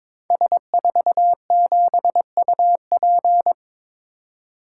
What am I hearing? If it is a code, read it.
Morse code, heard as a keyed tone, about 22 wpm, 700 Hz. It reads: S47UP
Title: Code Groups - 5 Letter:Numbers - 22wpm